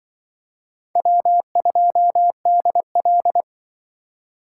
Morse code, heard as a keyed tone, about 23 wpm, 700 Hz. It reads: W2DL
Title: Speed Racing - USA 1x2 Call Signs 20wpm